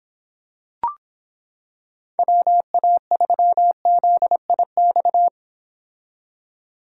Morse code, heard as a keyed tone, about 26 wpm, 700 Hz. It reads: WA3ZIX